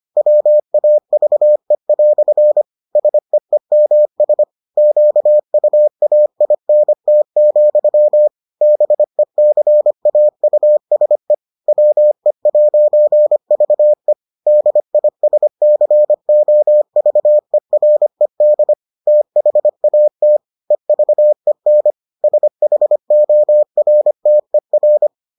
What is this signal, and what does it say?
Morse code, heard as a keyed tone, about 25 wpm, 600 Hz. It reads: WAVE" SEEMS QUAINT, BECAUSE WE'VE DISCOVERED THAT EVEN SHORTER